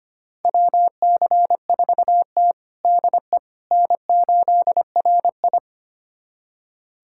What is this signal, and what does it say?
Morse code, heard as a keyed tone, about 25 wpm, 700 Hz. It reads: WC4T DE N8RI
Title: USA Callsign DE Callsign - 4 Characters - 25wpm